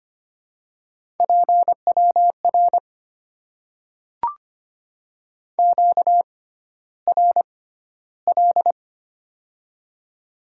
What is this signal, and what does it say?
Morse code, heard as a keyed tone, about 25 wpm, spaced as for 13 wpm, 700 Hz. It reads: PWR QRL
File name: Top 90 QSO Elements - 6x Character Spacing - 25wpm